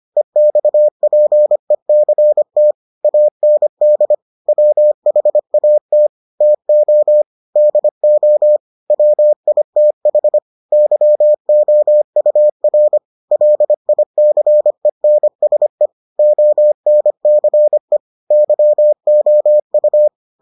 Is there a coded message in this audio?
Yes: Morse code from 0.2 to 20.1 s, heard as a keyed tone, about 25 wpm, 600 Hz. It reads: EXPECT AND WHAT TO DO WITH YOUR LICENSE ONCE YOU